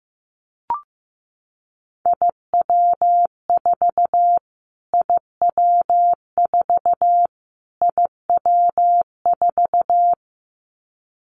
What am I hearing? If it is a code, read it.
Morse code, heard as a keyed tone, about 15 wpm, 700 Hz. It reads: IW4 IW4 IW4
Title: Callsign Prefixes for Top 25 DXCC - Repeated 3x - 15wpm